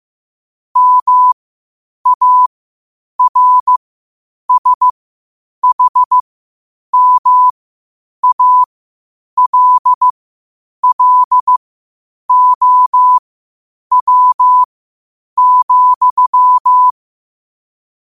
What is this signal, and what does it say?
Morse code, heard as a keyed tone, about 15 wpm, spaced as for 8 wpm, 1000 Hz. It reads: MARSHMALLOW,
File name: qotdfiles / 8wpm